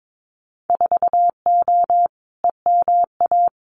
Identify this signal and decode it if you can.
Morse code, heard as a keyed tone, about 22 wpm, 700 Hz. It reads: 4O EMA
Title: Winter Field Day Exchanges 22wpm